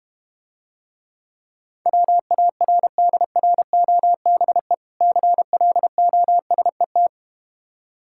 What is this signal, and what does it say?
Morse code, heard as a keyed tone, about 31 wpm, 700 Hz. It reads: WARDROBE CLOSET
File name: Difficult Words - Speed-Racing with Hints - 25wpm